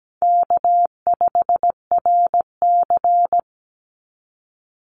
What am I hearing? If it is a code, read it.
Morse code, heard as a keyed tone, about 17 wpm, 700 Hz. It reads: K5RC
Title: Easy Call Signs Repeated 2x - 17wpm